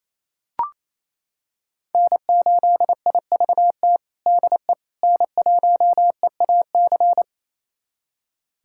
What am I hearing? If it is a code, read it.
Morse code, heard as a keyed tone, about 27 wpm, 700 Hz. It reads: N8IVT DE N1EAC
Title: USA Callsign DE Callsign - 5 Characters - 28wpm